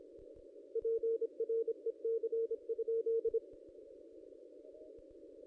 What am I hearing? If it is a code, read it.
Morse code, heard as a keyed tone, about 29 wpm, 450 Hz. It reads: PREC?
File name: contests / ss-cw-k5tr-op-n5rz -1500